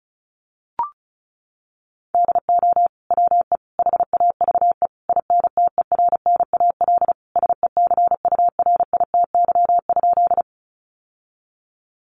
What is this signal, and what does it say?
Morse code, heard as a keyed tone, about 34 wpm, 700 Hz. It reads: DO WE HAVE INTERNAL SECURITY?
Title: Sentences from Top 1400 Words 35wpm